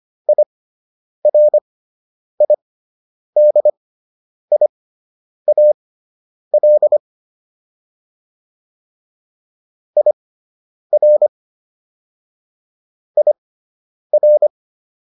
Morse code, heard as a keyed tone, about 25 wpm, spaced as for 9 wpm, 600 Hz. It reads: IRIDIAL IR IR